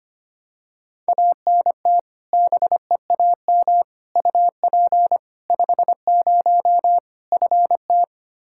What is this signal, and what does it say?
Morse code, heard as a keyed tone, about 25 wpm, 700 Hz. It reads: ANT BEAM UP 50 FT
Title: K1USN SST QSO Practice 25wpm